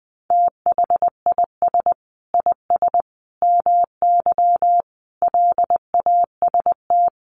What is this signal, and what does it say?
Morse code, heard as a keyed tone, about 20 wpm, 700 Hz. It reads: THIS IS MY LAST
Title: Sentences from Top 1300 Words 20wpm